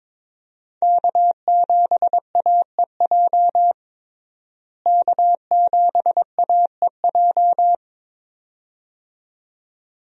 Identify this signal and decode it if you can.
Morse code, heard as a keyed tone, about 22 wpm, 700 Hz. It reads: K7AEJ K7AEJ
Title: USA Call Signs - 5 Characters - Repeated 2x - 22wpm